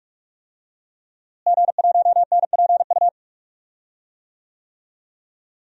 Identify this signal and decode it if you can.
Morse code, heard as a keyed tone, about 45 wpm, 700 Hz. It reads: G1NPU
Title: DXCC-100 Call Signs - 5 Characters - Repeated 2x - 45wpm